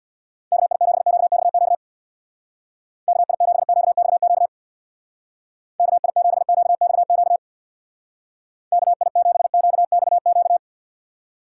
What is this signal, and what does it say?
Morse code, heard as a keyed tone, about 62 wpm, 700 Hz. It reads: KI7QXQ KI7QXQ KI7QXQ KI7QXQ
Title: Speed Racing - USA 2x3 Call Signs 50wpm